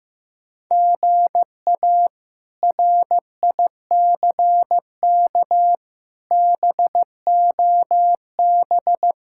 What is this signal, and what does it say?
Morse code, heard as a keyed tone, about 15 wpm, 700 Hz. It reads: GA RICK BOB
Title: K1USN SST QSO Practice - 15wpm